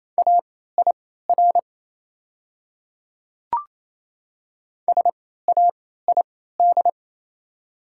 Morse code, heard as a keyed tone, about 28 wpm, spaced as for 16 wpm, 700 Hz. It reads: AIR SAID
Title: 3x Character Spacing - Top 200 Words - 28wpm